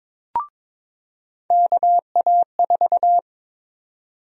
Morse code, heard as a keyed tone, about 22 wpm, 700 Hz. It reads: KA4
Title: USA Call Sign Prefixes - 22wpm